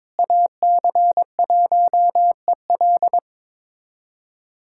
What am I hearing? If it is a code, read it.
Morse code, heard as a keyed tone, about 22 wpm, 700 Hz. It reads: AC1EL
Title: USA Callsign DE Callsign 22wpm